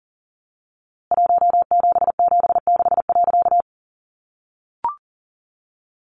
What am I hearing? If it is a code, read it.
Morse code, heard as a keyed tone, about 40 wpm, 700 Hz. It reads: 1776.
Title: A Sound Of Thunder 40wpm